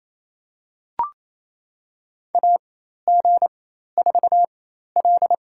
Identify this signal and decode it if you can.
Morse code, heard as a keyed tone, about 29 wpm, spaced as for 13 wpm, 700 Hz. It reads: AG4L
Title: USA Call Signs - 4 Characters - 4x Character Spacing - 28wpm